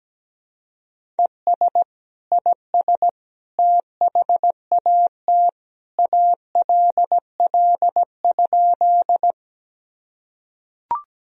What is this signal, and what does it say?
Morse code, heard as a keyed tone, about 17 wpm, 700 Hz. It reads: ES IS THAT ALL?
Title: Sentences from Top 100 Words and Abbreviations 17wpm